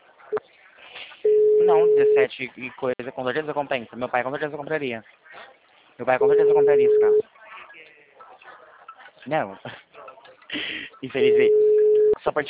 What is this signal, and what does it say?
Ringback tone (Europe, 425 Hz), with speech.